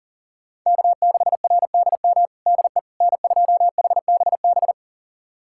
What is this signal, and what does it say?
Morse code, heard as a keyed tone, about 40 wpm, 700 Hz. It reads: K6RDM DE N2HBB
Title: USA Callsign DE Callsign - 5 Characters - 40wpm